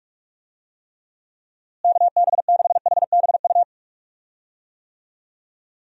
Morse code, heard as a keyed tone, about 46 wpm, 700 Hz. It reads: KB6HBU
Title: USA Call Signs - 6 Characters - Repeated 2x - 45wpm